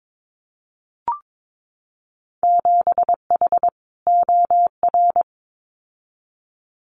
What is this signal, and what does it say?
Morse code, heard as a keyed tone, about 22 wpm, 700 Hz. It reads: 7H OR
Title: Winter Field Day Exchanges Encore 22wpm